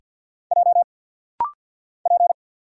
Morse code, heard as a keyed tone, about 50 wpm, 700 Hz. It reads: J P
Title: Single Letters - Rapid-Fire 50wpm